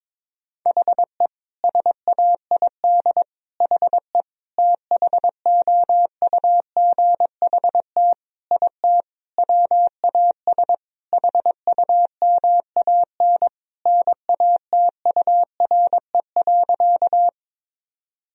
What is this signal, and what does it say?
Morse code, heard as a keyed tone, about 22 wpm, 700 Hz. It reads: HE SAID HE THOUGHT IT WAS HUMAN NATURE.
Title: Sentences from Top 1000 Words - Review Part 3 - 22wpm